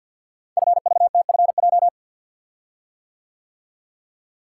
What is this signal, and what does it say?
Morse code, heard as a keyed tone, about 48 wpm, 700 Hz. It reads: FVTFJ
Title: Code Groups - 5 Letters - 50wpm